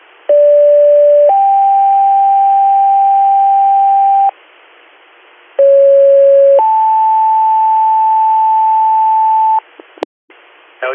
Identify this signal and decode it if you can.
Two-tone page, heard as a one-second tone, then a three-second tone, about 580 then 790 Hz, about 550 then 880 Hz, with speech. It is heard from 0.3 to 9.6 s.